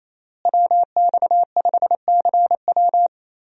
Morse code, heard as a keyed tone, about 28 wpm, 700 Hz. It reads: WX5CW
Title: USA Call Signs - 5 Characters - Repeated 2x - 28wpm